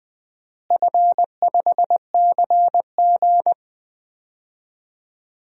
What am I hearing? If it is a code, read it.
Morse code, heard as a keyed tone, about 20 wpm, 700 Hz. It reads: F5CG
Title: DXCC-100 Call Signs - 4 Characters - Repeated 3x - 20wpm